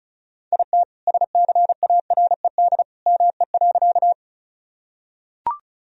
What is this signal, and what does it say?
Morse code, heard as a keyed tone, about 36 wpm, 700 Hz. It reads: IT SCARED ME.
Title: Sentences from Top 1000 Words - Review Part 4 - 35wpm